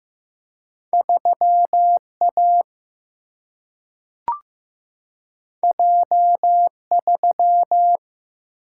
Morse code, heard as a keyed tone, about 15 wpm, 700 Hz. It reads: 3A J3
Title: Single-Letter Single-Number 15wpm